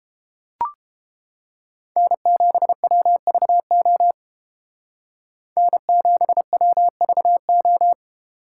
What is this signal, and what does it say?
Morse code, heard as a keyed tone, about 30 wpm, 700 Hz. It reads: N7WVO N7WVO